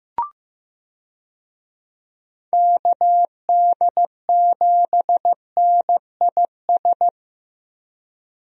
Morse code, heard as a keyed tone, about 15 wpm, 700 Hz. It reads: KD7NIS
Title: USA Call Signs - 6 Characters - Repeated 2x - 15wpm